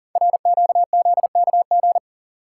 Morse code, heard as a keyed tone, about 42 wpm, 700 Hz. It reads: RQZKG
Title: Code Groups - 5 Letter:Numbers - 40wpm